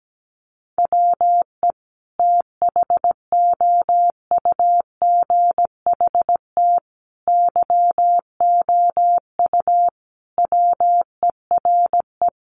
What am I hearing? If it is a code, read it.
Morse code, heard as a keyed tone, about 17 wpm, 700 Hz. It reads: WE THOUGHT YOU WERE
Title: Sentences from Top 800 Words 17wpm